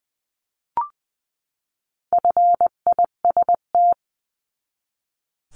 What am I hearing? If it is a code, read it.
Morse code, heard as a keyed tone, about 19 wpm, 700 Hz. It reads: FIST
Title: 259 Lesson 42d - Words 18-20wpm